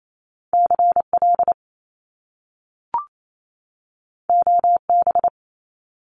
Morse code, heard as a keyed tone, about 28 wpm, 700 Hz. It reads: CL OB
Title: Top Two Letter Combinations in Words 28wpm